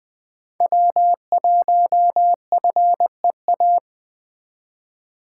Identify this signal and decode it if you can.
Morse code, heard as a keyed tone, about 20 wpm, 700 Hz. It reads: W1FEA